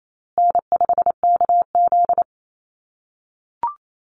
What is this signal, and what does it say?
Morse code, heard as a keyed tone, about 28 wpm, 700 Hz. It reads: N5KZ
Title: USA Call Signs Repeated 3x 28wpm